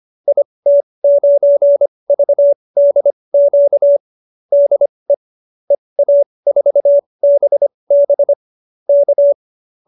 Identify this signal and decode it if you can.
Morse code, heard as a keyed tone, about 25 wpm, spaced as for 20 wpm, 570 Hz. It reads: IT9VDQ DE EA4BB K
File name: qso215b 20